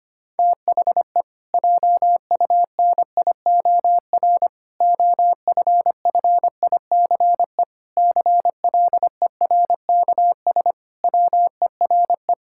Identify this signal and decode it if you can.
Morse code, heard as a keyed tone, about 25 wpm, 700 Hz. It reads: THE JUNIOR OFFICE CLERKS WERE
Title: Pangram Challenge v2 - 25wpm